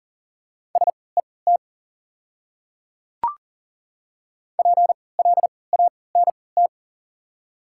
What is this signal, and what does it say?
Morse code, heard as a keyed tone, about 40 wpm, spaced as for 23 wpm, 700 Hz. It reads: SET PLANT
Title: 3x Character Spacing - Top 500 Words - 40wpm